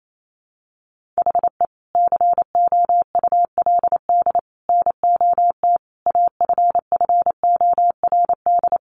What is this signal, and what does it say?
Morse code, heard as a keyed tone, about 28 wpm, 700 Hz. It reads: HE COULD NOT AFFORD
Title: Sentences from Top 1100 Words 28wpm